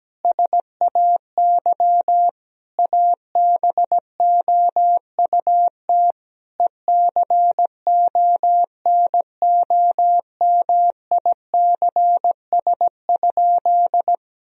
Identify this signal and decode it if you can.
Morse code, heard as a keyed tone, about 17 wpm, 700 Hz. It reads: SAY ABOUT ECONOMICS?
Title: Sentences from Top 1000 Words - Review Part 3 - 17wpm